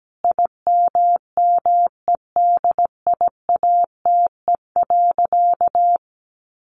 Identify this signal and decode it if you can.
Morse code, heard as a keyed tone, about 17 wpm, 700 Hz. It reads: IMMEDIATE.
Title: Sentences from Top 1200 Words 17wpm